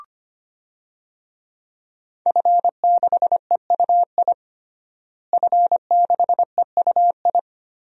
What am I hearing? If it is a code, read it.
Morse code, heard as a keyed tone, about 25 wpm, 700 Hz. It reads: F6EUI F6EUI
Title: DXCC-100 Call Signs - 5 Characters - Repeated 2x - 25wpm